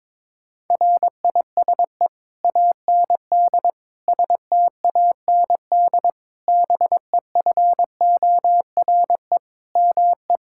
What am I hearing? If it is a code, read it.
Morse code, heard as a keyed tone, about 22 wpm, 700 Hz. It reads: RISE AND STAND BEFORE ME
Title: Sentences from Top 1600 Words 22wpm